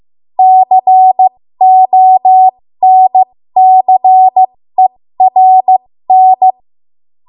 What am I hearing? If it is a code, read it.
Morse code, heard as a keyed tone, about 15 wpm, spaced as for 13 wpm, 750 Hz. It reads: CONCERN
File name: Morse / 13WPM